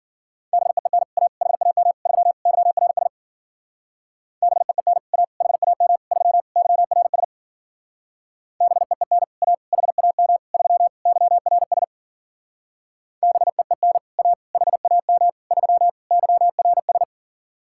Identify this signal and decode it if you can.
Morse code, heard as a keyed tone, about 47 wpm, 700 Hz. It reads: BEEN A HAM 3 YRS BEEN A HAM 3 YRS BEEN A HAM 3 YRS BEEN A HAM 3 YRS
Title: Speed-Racing Common QSO Phrases - 40wpm